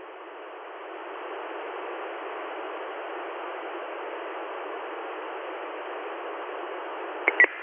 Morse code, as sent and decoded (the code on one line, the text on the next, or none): none